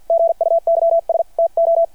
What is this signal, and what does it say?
Morse code, heard as a keyed tone, about 51 wpm, spaced as for 45 wpm, 660 Hz. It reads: GUYS TO
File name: archived / CRSnet